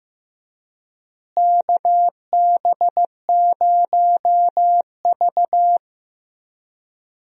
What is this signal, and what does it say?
Morse code, heard as a keyed tone, about 15 wpm, 700 Hz. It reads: KB0V
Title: USA Call Signs - 4 Characters - Repeated 3x - 15wpm